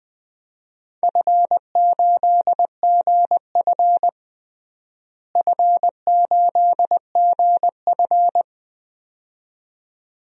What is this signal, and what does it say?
Morse code, heard as a keyed tone, about 20 wpm, 700 Hz. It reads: F8GF F8GF
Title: DXCC-100 Call Signs - 4 Characters - Repeated 2x - 20wpm